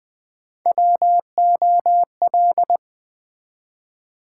Morse code, heard as a keyed tone, about 20 wpm, 700 Hz. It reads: WOL
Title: Build A Word - 1 Letter At A Time - 4-Letter Words - 20wpm